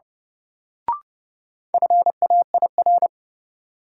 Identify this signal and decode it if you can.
Morse code, heard as a keyed tone, about 30 wpm, 700 Hz. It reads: FAIR
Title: Sentences from Top 1000 Words - Review Part 5 - New Word Drill - 30wpm